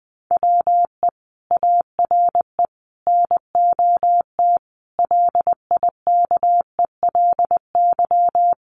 Morse code, heard as a keyed tone, about 20 wpm, 700 Hz. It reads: WE ARE NOT LIKELY
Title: Sentences from Top 1100 Words 20wpm